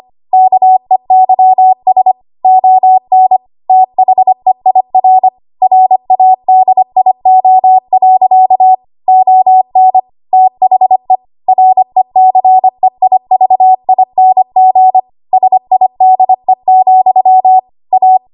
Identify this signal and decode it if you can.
Morse code, heard as a keyed tone, about 25 wpm, 750 Hz. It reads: KEYS ON THEIR RADIO. ON THE RECEIVING SIDE, A
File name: Morse / 25WPM